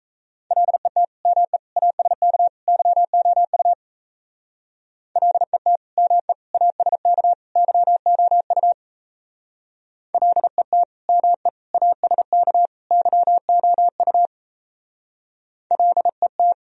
Morse code, heard as a keyed tone, about 36 wpm, 700 Hz. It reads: LET ME ASK YOU LET ME ASK YOU LET ME ASK YOU LET